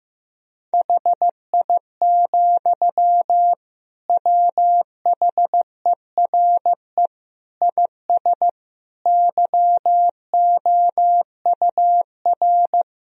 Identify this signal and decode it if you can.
Morse code, heard as a keyed tone, about 15 wpm, 700 Hz. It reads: HI, WHERE IS YOUR
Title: Sentences from Top 1000 Words - Review Part 3 - 15wpm